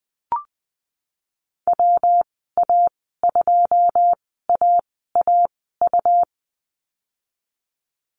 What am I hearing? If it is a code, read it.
Morse code, heard as a keyed tone, about 20 wpm, spaced as for 15 wpm, 700 Hz. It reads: WA2AAU
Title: 147 Lesson 24g - Letter y - Call Signs 20wpm